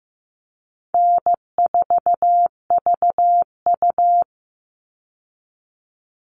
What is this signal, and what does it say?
Morse code, heard as a keyed tone, about 15 wpm, 700 Hz. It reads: N4VU
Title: All Cut Numbers - 4-Digit 15wpm